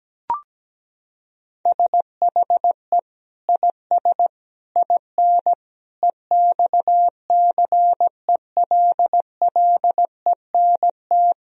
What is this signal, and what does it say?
Morse code, heard as a keyed tone, about 17 wpm, 700 Hz. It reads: SHE IS IN EXCELLENT
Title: Sentences from Top 1000 Words - Review Part 3 - 17wpm